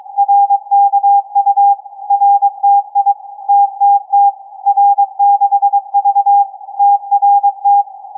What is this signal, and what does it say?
Morse code, heard as a keyed tone, about 22 wpm, 810 Hz. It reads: RKU RTI TTT R6V TRT
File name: CAS4A-TURN-AIRSPY-01